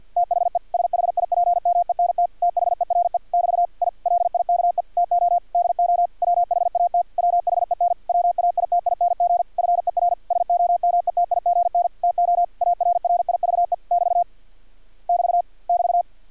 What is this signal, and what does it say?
Morse code, heard as a keyed tone, about 50 wpm, 700 Hz. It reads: THE SHIPMENT THERE <BT> I LIKE TO DO WHAT WHEN WAITING FER SOMETIGN TO ARRIVE <BT> <BT> <BT>
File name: archived / CRSnet